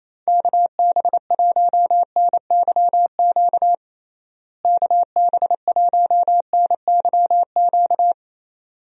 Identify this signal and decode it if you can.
Morse code, heard as a keyed tone, about 28 wpm, 700 Hz. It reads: KB1NYQ KB1NYQ
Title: USA Call Signs - 6 Characters - Repeated 2x - 28wpm